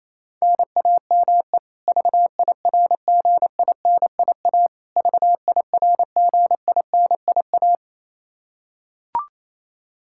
Morse code, heard as a keyed tone, about 28 wpm, 700 Hz. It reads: NAME VIRGINIA VIRGINIA
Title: Practice QSO - Exchange 1 - Beginning - 28wpm